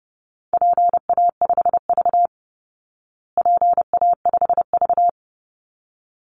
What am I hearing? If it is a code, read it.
Morse code, heard as a keyed tone, about 30 wpm, 700 Hz. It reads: PA5V PA5V
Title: Call Signs Repeated 3x - 30wpm